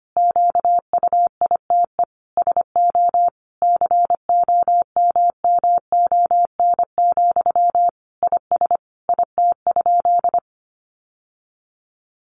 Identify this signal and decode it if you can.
Morse code, heard as a keyed tone, about 25 wpm, 700 Hz. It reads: QUITE SO COMMON, IS IT?
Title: Sentences from Top 700 Words 25wpm